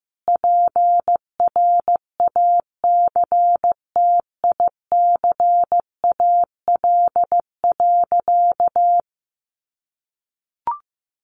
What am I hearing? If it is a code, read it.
Morse code, heard as a keyed tone, about 14 wpm, 700 Hz. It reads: PRACTICAL.
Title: Sentences from Top 1400 Words 15wpm